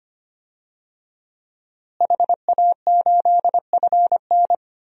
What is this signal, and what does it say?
Morse code, heard as a keyed tone, about 25 wpm, 700 Hz. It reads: HA8FN
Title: DXCC-100 Call Signs - 5 Characters - Repeated 3x - 25wpm